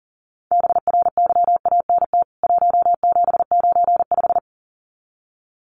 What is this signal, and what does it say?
Morse code, heard as a keyed tone, about 40 wpm, 700 Hz. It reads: BRYANT 1795
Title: CWT Exchange 40wpm